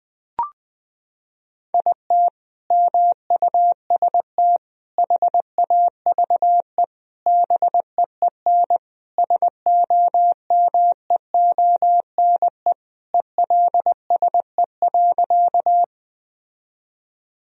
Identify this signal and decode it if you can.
Morse code, heard as a keyed tone, about 20 wpm, 700 Hz. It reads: IT MUST HAVE BEEN SOMEONE ELSE.
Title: Sentences from Top 1000 Words - Review Part 4 - 20wpm